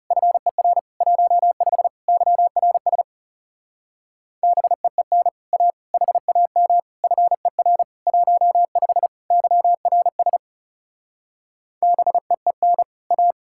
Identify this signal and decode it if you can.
Morse code, heard as a keyed tone, about 37 wpm, 700 Hz. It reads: FER 15 YRS BEEN A HAM FER 15 YRS BEEN A
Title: Speed-Racing Common QSO Phrases Encore - 30wpm